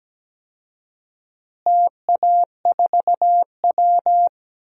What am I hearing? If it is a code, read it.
Morse code, heard as a keyed tone, about 17 wpm, 700 Hz. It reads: TA4W
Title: DXCC-100 Call Signs - 4 Characters - Repeated 3x - 17wpm